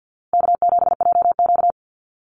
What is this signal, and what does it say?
Morse code, heard as a keyed tone, about 49 wpm, 700 Hz. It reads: K7JQ
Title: Call Signs Repeated 3x - 50wpm